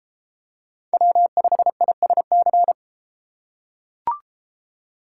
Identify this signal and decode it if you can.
Morse code, heard as a keyed tone, about 33 wpm, 700 Hz. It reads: W5ISC